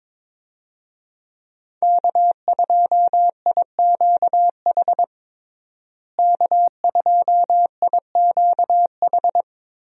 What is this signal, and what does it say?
Morse code, heard as a keyed tone, about 22 wpm, 700 Hz. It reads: K2IQH K2IQH
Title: USA Call Signs - 5 Characters - Repeated 3x - 22wpm